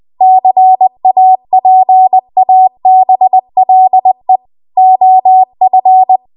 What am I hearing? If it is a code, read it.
Morse code, heard as a keyed tone, about 20 wpm, 750 Hz. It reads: CAPABLE OF
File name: Morse / 20WPM